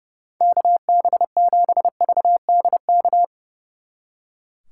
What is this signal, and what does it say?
Morse code, heard as a keyed tone, about 37 wpm, 700 Hz. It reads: KB7VDK
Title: Speed Racing - USA 2x3 Call Signs 30wpm